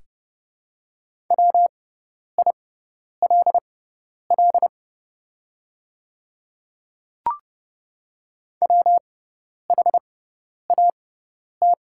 Morse code, heard as a keyed tone, about 30 wpm, spaced as for 10 wpm, 690 Hz. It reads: WILL WHAT
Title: 6x Character Spacing - Top 100 Words - 30wpm